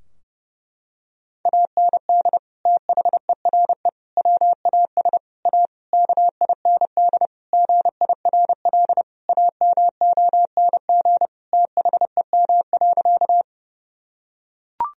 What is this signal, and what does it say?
Morse code, heard as a keyed tone, about 30 wpm, 700 Hz. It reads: AND THERE WAS A KIND GIRL AMONG THEM.
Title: Sentences from Top 1000 Words - Review Part 3 - 30wpm